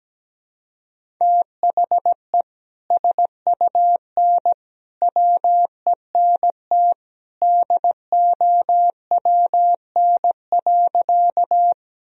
Morse code, heard as a keyed tone, about 17 wpm, 700 Hz. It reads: THE SUN WENT DOWN.
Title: Sentences from Top 1000 Words - Review Part 1 - 17wpm